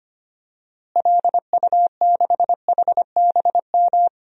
Morse code, heard as a keyed tone, about 25 wpm, 700 Hz. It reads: LU6HBM
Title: DXCC-100 Call Signs - 6 Characters - Repeated 2x - 25wpm